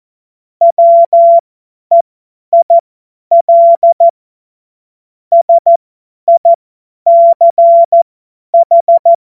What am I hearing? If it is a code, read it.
Morse code, heard as a keyed tone, about 14 wpm, spaced as for 10 wpm, 680 Hz. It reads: WEIL SICH
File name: Morsekurs / Witze / Witz 29. Juni 2021, 50 F70